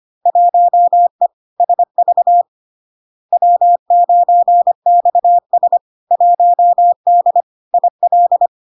Morse code, heard as a keyed tone, about 25 wpm, 700 Hz. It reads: 1E SV W9XS 1D IL
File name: LARK-FD-2022-Log-in-CW-25wpm-700-Hz